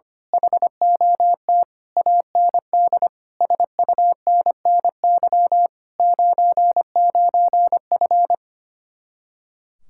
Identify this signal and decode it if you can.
Morse code, heard as a keyed tone, about 24 wpm, 700 Hz. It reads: HOT AND SUNNY 99F
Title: Speed-Racing Common QSO Phrases - 25wpm